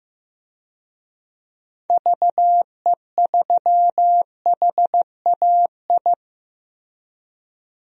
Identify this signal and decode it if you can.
Morse code, heard as a keyed tone, about 15 wpm, 700 Hz. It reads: VE3HAI
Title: DXCC-100 Call Signs - 6 Characters - Repeated 2x - 15wpm